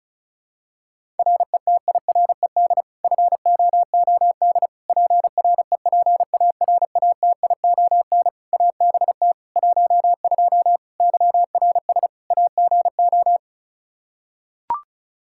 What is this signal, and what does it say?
Morse code, heard as a keyed tone, about 36 wpm, 700 Hz. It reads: RETIRED FOOD PREPARATION ABT 12 YRS AGO
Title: K1USN SST QSO Practice 35wpm